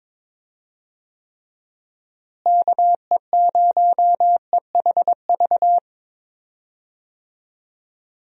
Morse code, heard as a keyed tone, about 22 wpm, 700 Hz. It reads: KE0EHV
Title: USA Call Signs - 6 Characters - Repeated 2x - 22wpm